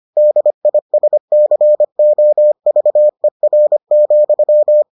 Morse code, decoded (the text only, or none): DISCOVER,